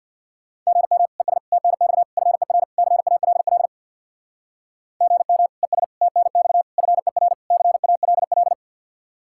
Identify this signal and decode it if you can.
Morse code, heard as a keyed tone, about 60 wpm, 700 Hz. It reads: GM ES TNX FER CALL GM ES TNX FER CALL
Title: Speed-Racing Common QSO Phrases - 50wpm